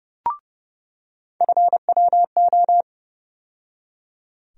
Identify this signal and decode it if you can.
Morse code, heard as a keyed tone, about 30 wpm, 700 Hz. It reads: FWO
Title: USA Call Sign Postfixes - 3 Letters 30wpm